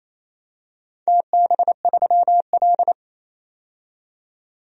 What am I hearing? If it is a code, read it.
Morse code, heard as a keyed tone, about 28 wpm, 700 Hz. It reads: TB3L
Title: DXCC-100 Call Signs - 4 Characters - 28wpm